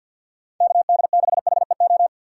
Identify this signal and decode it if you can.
Morse code, heard as a keyed tone, about 47 wpm, 700 Hz. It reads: KD6HEO
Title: USA Call Signs - 4x Character Spacing - 50wpm